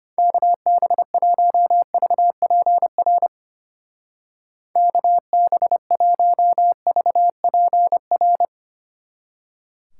Morse code, heard as a keyed tone, about 31 wpm, 700 Hz. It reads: KB1VPR KB1VPR
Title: Speed Racing - USA 2x3 Call Signs 25wpm